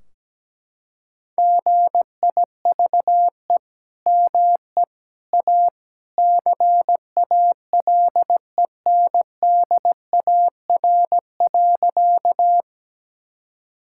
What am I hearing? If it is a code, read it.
Morse code, heard as a keyed tone, about 17 wpm, 700 Hz. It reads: GIVE ME A CALENDAR.
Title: Sentences from Top 1600 Words 17wpm